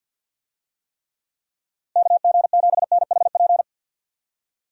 Morse code, heard as a keyed tone, about 49 wpm, 700 Hz. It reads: KG7NHP
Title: USA Call Signs - 6 Characters - Repeated 3x - 50wpm